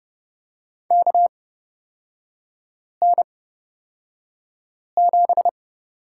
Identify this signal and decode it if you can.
Morse code, heard as a keyed tone, about 30 wpm, spaced as for 5 wpm, 700 Hz. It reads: KN7
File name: Lesson 36e - BT - Three Characters 30wpm